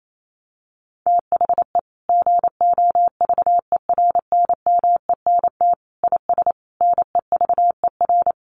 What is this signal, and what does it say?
Morse code, heard as a keyed tone, about 28 wpm, 700 Hz. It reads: THE GOVERNMENT IS NEVER
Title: Sentences from Top 1300 Words 28wpm